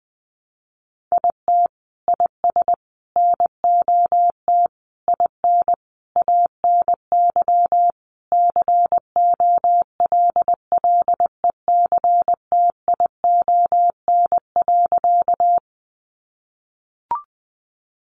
Morse code, heard as a keyed tone, about 20 wpm, 700 Hz. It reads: IT IS NOT IN ANY COLLECTION.
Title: Sentences from Top 1200 Words 20wpm